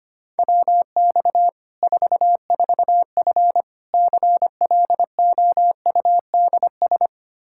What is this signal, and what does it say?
Morse code, heard as a keyed tone, about 25 wpm, 700 Hz. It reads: WX 44F CLOUDS
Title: K1USN SST QSO Practice 25wpm